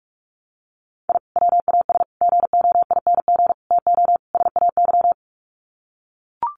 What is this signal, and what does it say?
Morse code, heard as a keyed tone, about 44 wpm, 700 Hz. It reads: I WAS GOING TO SAY
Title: Sets of 5 Words 45wpm